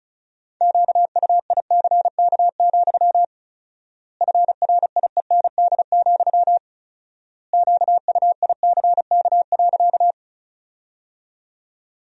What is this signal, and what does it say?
Morse code, heard as a keyed tone, about 36 wpm, 700 Hz. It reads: QUICK, FRIEND, QUICK.
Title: 4x Word-Spacing - Sentences from Top 500 Words - 35wpm